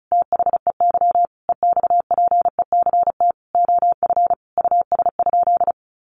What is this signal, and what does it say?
Morse code, heard as a keyed tone, about 34 wpm, 700 Hz. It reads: THEY EXPECT OF US?
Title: Sentences from Top 1400 Words 35wpm